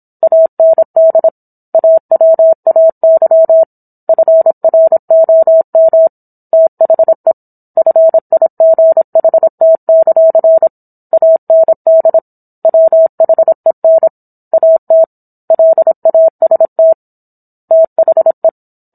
Morse code, heard as a keyed tone, about 27 wpm, 650 Hz. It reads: AND AWAY FROM THE FIGHT; AND WHEN AT LAST THE